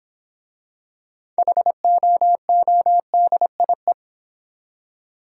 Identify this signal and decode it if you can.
Morse code, heard as a keyed tone, about 25 wpm, 700 Hz. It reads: HOODIE
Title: Easy Words - Speed-Racing with Hints - 20wpm